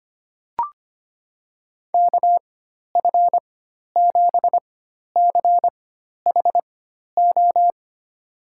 Morse code, heard as a keyed tone, about 25 wpm, spaced as for 12 wpm, 700 Hz. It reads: KF7CHO
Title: USA Call Signs - 4x Character Spacing - 25wpm